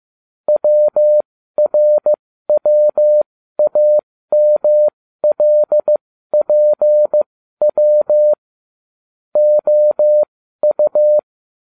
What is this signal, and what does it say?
Morse code, heard as a keyed tone, about 15 wpm, spaced as for 12 wpm, 600 Hz. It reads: WRWAMLPW OU